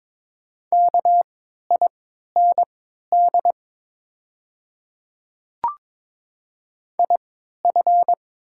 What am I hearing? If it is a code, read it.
Morse code, heard as a keyed tone, about 23 wpm, spaced as for 13 wpm, 690 Hz. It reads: KIND IF
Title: 3x Character Spacing - Top 300 Words - 22wpm